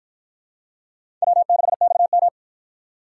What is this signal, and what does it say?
Morse code, heard as a keyed tone, about 53 wpm, 700 Hz. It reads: W6XM